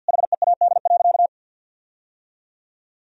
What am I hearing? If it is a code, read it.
Morse code, heard as a keyed tone, about 50 wpm, 700 Hz. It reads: HEAD.
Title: Sentences from Top 1000 Words - Review Part 3 - 50wpm